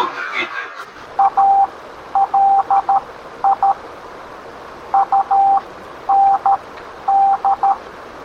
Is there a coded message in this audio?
Yes: Morse code from 1.2 to 7.7 s, heard as a keyed tone, about 13 wpm, spaced as for 10 wpm, 790 Hz. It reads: ALI UND